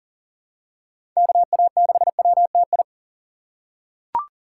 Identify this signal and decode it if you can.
Morse code, heard as a keyed tone, about 40 wpm, 700 Hz. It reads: KA6WTI
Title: USA Call Signs - 6 Characters - 6x Character Spacing - 40wpm